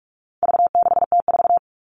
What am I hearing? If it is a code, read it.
Morse code, heard as a keyed tone, about 44 wpm, 700 Hz. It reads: V6T4
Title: All Cut Numbers - 4-Digit 45wpm